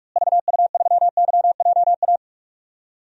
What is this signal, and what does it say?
Morse code, heard as a keyed tone, about 45 wpm, 700 Hz. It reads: VU3YJA
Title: DXCC-100 Call Signs - 6 Characters - Repeated 2x - 45wpm